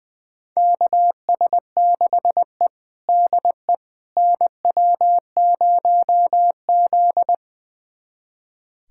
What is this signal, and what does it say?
Morse code, heard as a keyed tone, about 20 wpm, 700 Hz. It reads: KS6E DE NW0Z
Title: USA Callsign DE Callsign - 4 Characters - 20wpm